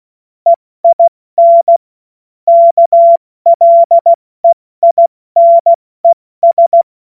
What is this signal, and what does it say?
Morse code, heard as a keyed tone, about 16 wpm, spaced as for 14 wpm, 680 Hz. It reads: EIN KLEINES
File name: Morsekurs / Witze / Witz 28.November 2023 70 F80